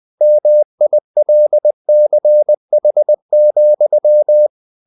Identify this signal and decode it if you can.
Morse code, heard as a keyed tone, about 20 wpm, 600 Hz. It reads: MILCH,